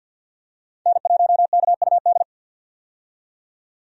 Morse code, heard as a keyed tone, about 47 wpm, 700 Hz. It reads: N1KUD
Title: USA Call Signs - 5 Characters - 50wpm